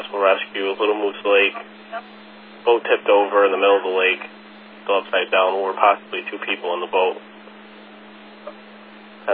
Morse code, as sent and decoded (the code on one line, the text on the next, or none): none